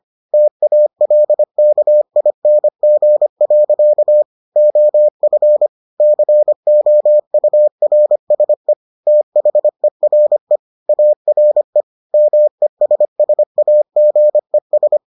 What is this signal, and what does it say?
Morse code, heard as a keyed tone, about 25 wpm, 600 Hz. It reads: TALKING. OF COURSE THERE ARE MESSAGES